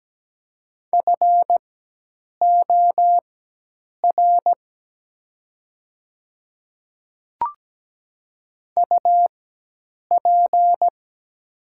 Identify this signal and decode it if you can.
Morse code, heard as a keyed tone, about 17 wpm, spaced as for 8 wpm, 700 Hz. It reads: FOR UP
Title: 4x Character Spacing - Top 100 Words - 17wpm